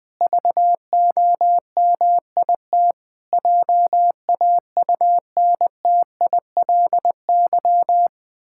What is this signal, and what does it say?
Morse code, heard as a keyed tone, about 20 wpm, 700 Hz. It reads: VOMIT JAUNTILY
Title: Pangram Challenge v2 - 20wpm